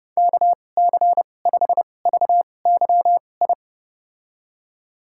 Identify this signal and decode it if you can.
Morse code, heard as a keyed tone, about 30 wpm, spaced as for 22 wpm, 700 Hz. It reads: KC5VYI
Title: USA Call Signs - 6 Characters - 2x Character Spacing - 30wpm